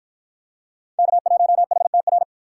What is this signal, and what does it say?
Morse code, heard as a keyed tone, about 53 wpm, 700 Hz. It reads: K1HTR